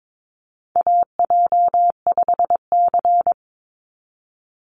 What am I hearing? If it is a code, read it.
Morse code, heard as a keyed tone, about 22 wpm, 700 Hz. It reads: AJ5C
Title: NA SOTA Chaser Callsigns 22wpm